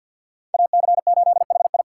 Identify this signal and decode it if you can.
Morse code, heard as a keyed tone, about 48 wpm, 700 Hz. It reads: AC8HI
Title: USA Call Signs - 2x Character Spacing - 50wpm